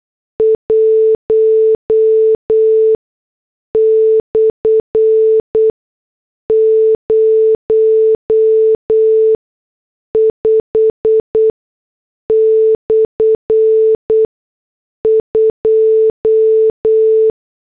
Morse code, heard as a keyed tone, about 8 wpm, spaced as for 6 wpm, 440 Hz. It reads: 1/05/2
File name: morse / headlines